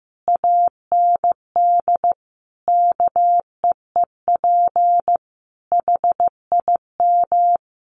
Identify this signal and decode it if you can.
Morse code, heard as a keyed tone, about 15 wpm, 700 Hz. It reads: AND KEEP HIM
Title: Sentences from Top 1500 Words 15wpm